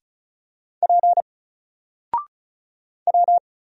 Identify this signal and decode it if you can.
Morse code, heard as a keyed tone, about 33 wpm, 700 Hz. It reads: P W
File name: Single-Letter-Number-35wpm